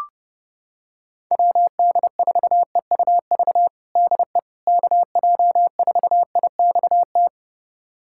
Morse code, heard as a keyed tone, about 32 wpm, 700 Hz. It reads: WD4EUV DE KJ4IXT
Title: USA Callsign DE Callsign - 6 Characters - 30wpm